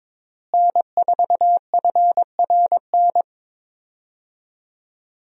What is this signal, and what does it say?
Morse code, heard as a keyed tone, about 22 wpm, 700 Hz. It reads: N4FRN
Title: USA Call Signs - 5 Characters - Repeated 3x - 22wpm